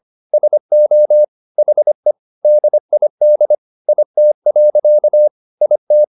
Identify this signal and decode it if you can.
Morse code, heard as a keyed tone, about 25 wpm, 600 Hz. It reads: SO HE DID IT. IT
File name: the old man and the sea part 1 25wpm